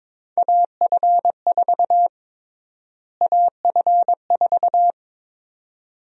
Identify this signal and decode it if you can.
Morse code, heard as a keyed tone, about 22 wpm, 700 Hz. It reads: AF4 AF4
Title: USA Call Sign Prefixes - Repeated 2x - 22wpm